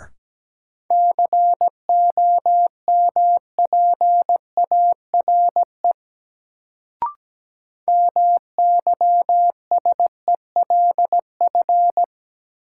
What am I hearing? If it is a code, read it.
Morse code, heard as a keyed tone, about 17 wpm, 700 Hz. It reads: COMPARE MYSELF
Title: Sentences from Top 1300 Words - New Word Drill Alt - 17wpm